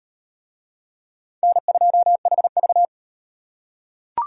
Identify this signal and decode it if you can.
Morse code, heard as a keyed tone, about 33 wpm, 700 Hz. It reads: N2HV